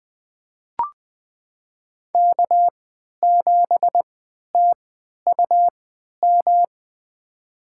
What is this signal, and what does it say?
Morse code, heard as a keyed tone, about 20 wpm, spaced as for 11 wpm, 700 Hz. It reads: K7TUM
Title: USA Call Signs - 5 Characters - 3x Character Spacing - 20wpm